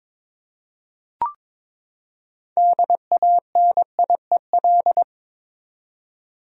Morse code, heard as a keyed tone, about 22 wpm, 700 Hz. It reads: DANIEL
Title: USA Common Names - 22wpm